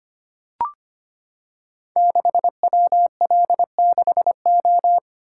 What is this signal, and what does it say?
Morse code, heard as a keyed tone, about 25 wpm, 700 Hz. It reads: 6WL6O
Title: Code Groups - 5 Letter:Numbers - 25wpm